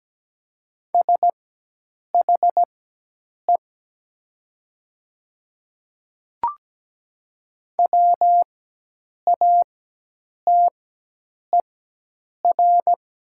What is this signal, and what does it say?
Morse code, heard as a keyed tone, about 17 wpm, spaced as for 8 wpm, 700 Hz. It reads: SHE WATER
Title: 4x Character Spacing - Top 100 Words - 17wpm